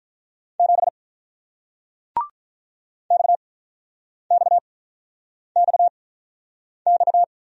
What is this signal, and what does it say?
Morse code, heard as a keyed tone, about 46 wpm, 700 Hz. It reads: 7 X X X X